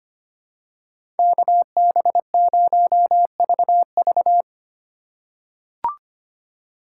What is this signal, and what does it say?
Morse code, heard as a keyed tone, about 25 wpm, 700 Hz. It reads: KB0VV
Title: USA Call Signs - 5 Characters - 2x Character Spacing - 25wpm